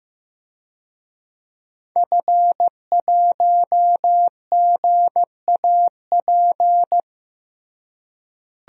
Morse code, heard as a keyed tone, about 15 wpm, 700 Hz. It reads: F1GAP
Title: DXCC-100 Call Signs - 5 Characters - Repeated 3x - 15wpm